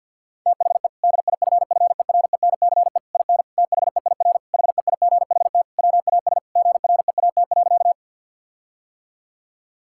Morse code, heard as a keyed tone, about 50 wpm, 700 Hz. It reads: THE DIFFERENCE IN THEIR HEIGHT WAS GREAT.
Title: Sentences from Top 1600 Words 50wpm